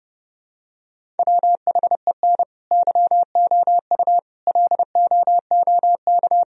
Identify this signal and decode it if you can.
Morse code, heard as a keyed tone, about 30 wpm, 700 Hz. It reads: WHEN YOU LOOK
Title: Sentences from Top 1000 Words - Review Part 1 - 30wpm